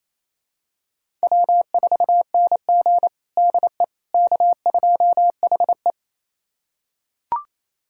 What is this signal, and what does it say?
Morse code, heard as a keyed tone, about 27 wpm, 690 Hz. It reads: W4NG DE K2HE
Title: USA Callsign DE Callsign - 4 Characters - 28wpm